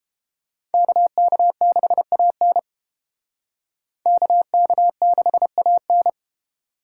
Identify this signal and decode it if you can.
Morse code, heard as a keyed tone, about 29 wpm, 700 Hz. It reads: KK6AN KK6AN